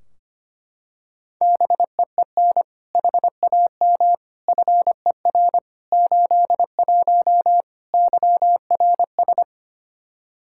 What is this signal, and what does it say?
Morse code, heard as a keyed tone, about 25 wpm, 700 Hz. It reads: BEEN HAM FER 81 YRS
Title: Practice QSO - Exchange 3 - More Info - 25wpm